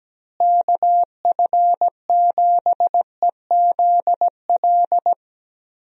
Morse code, heard as a keyed tone, about 17 wpm, 700 Hz. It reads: KF7EZL
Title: USA Call Signs - 6 Characters - Repeated 2x - 17wpm